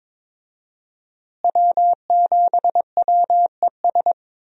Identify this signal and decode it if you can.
Morse code, heard as a keyed tone, about 21 wpm, 700 Hz. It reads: W7WES
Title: USA Call Signs - 5 Characters - Repeated 2x - 22wpm